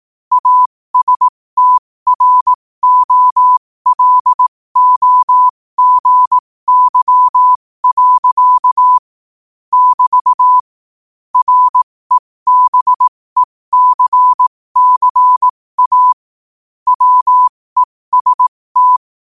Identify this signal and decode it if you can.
Morse code, heard as a keyed tone, about 18 wpm, spaced as for 15 wpm, 1000 Hz. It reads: ASTROLOGY. <BT> REBECCA WEST
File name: qotdfiles / 15wpm